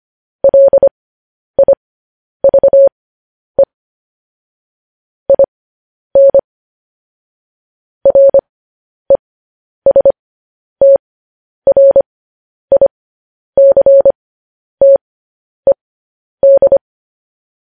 Morse code, heard as a keyed tone, about 25 wpm, spaced as for 10 wpm, 550 Hz. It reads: LIVE IN RESTRICTED
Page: news headlines in morse code at 10 wpm